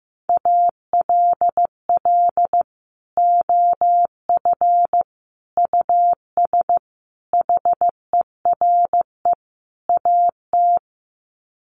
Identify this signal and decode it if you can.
Morse code, heard as a keyed tone, about 15 wpm, 700 Hz. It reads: ALL OF US HERE AT
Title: Sets of 5 Words 15wpm